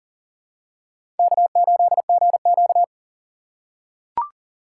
Morse code, heard as a keyed tone, about 40 wpm, 700 Hz. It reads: K8GQ
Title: USA Call Signs - 4 Characters - 2x Character Spacing - 40wpm